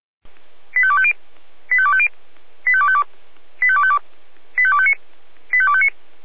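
Five-tone selective call (ZVEI), heard as quick runs of five tones, tones of about 70 ms. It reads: 85250 85250 85252 85252 85259 85259
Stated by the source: ZVEI five-tone call received by SDR.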